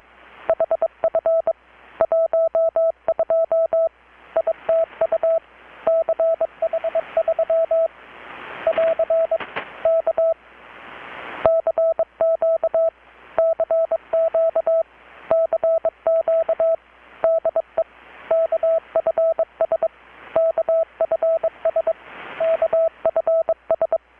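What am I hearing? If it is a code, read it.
Morse code, heard as a keyed tone, about 22 wpm, 650 Hz. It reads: HF 12 ITU CH3 <AR> K CQ CQ CQ DE KFS KFS KFS
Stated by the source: Morse from KPH's Night of Nights XX maritime radiotelegraphy event, received in Chiapas, Mexico about 3,800 km away on a Kenwood R-600 with a grounded 10 m long-wire antenna.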